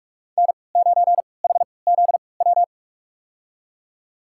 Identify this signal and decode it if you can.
Morse code, heard as a keyed tone, about 46 wpm, spaced as for 26 wpm, 700 Hz. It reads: N9HZW
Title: USA Call Signs - 5 Characters - 3x Character Spacing - 45wpm